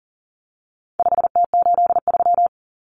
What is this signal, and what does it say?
Morse code, heard as a keyed tone, about 40 wpm, 700 Hz. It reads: 5T83
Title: Common Cut Numbers - 4-Digit 40wpm